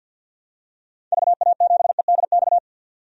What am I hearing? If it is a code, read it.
Morse code, heard as a keyed tone, about 52 wpm, 700 Hz. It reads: VA7EDX
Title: DXCC-100 Call Signs - 6 Characters - 50wpm